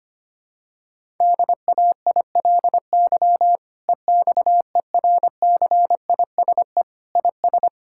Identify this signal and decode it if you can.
Morse code, heard as a keyed tone, about 25 wpm, 700 Hz. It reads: DAILY EXERCISE IS
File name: practice / 25wpm / Difficult Sentences with Hints - 25wpm